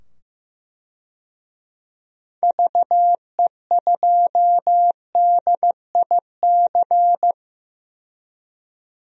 Morse code, heard as a keyed tone, about 15 wpm, 700 Hz. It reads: VE2DIC
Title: DXCC-100 Call Signs - 6 Characters - 15wpm